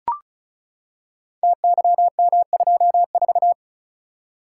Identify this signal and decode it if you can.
Morse code, heard as a keyed tone, about 36 wpm, 700 Hz. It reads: TYM24
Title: Code Groups - 5 Letter:Numbers - 35wpm